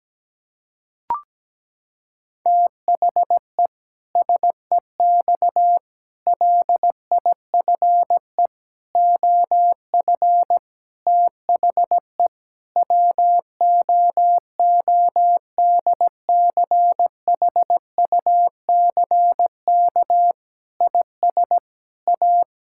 Morse code, heard as a keyed tone, about 17 wpm, 700 Hz. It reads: THE SEX LIFE OF THE WOODCHUCK IS A
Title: Pangram Challenge v2 - 17wpm